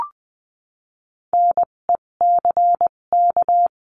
Morse code, heard as a keyed tone, about 20 wpm, spaced as for 17 wpm, 700 Hz. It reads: NECK
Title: 257 Lesson 42b - Words 14-20wpm